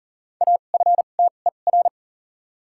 Morse code, heard as a keyed tone, about 40 wpm, spaced as for 29 wpm, 700 Hz. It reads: AFTER
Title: 2x Character Spacing - Top 200 Words - 40wpm